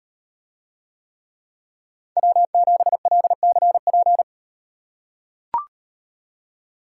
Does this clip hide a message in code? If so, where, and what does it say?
Yes: Morse code from 2.2 to 4.2 s, heard as a keyed tone, about 38 wpm, 700 Hz. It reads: W7LCP